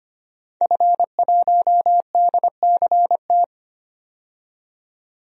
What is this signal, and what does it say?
Morse code, heard as a keyed tone, about 25 wpm, 690 Hz. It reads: F1DCT
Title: DXCC-100 Call Signs - 5 Characters - 25wpm